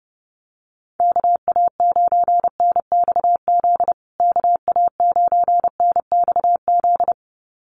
Morse code, heard as a keyed tone, about 30 wpm, 700 Hz. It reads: KA9NXZ KA9NXZ
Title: USA Call Signs Repeated 2x 30wpm